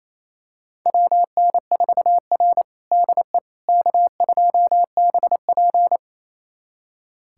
Morse code, heard as a keyed tone, about 28 wpm, 700 Hz. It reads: WN4R DE K2BP
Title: USA Callsign DE Callsign - 4 Characters - 28wpm